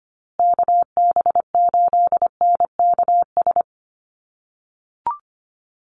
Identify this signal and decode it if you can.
Morse code, heard as a keyed tone, about 25 wpm, 700 Hz. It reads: KB8NKS
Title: USA Call Signs Repeated 3x 25wpm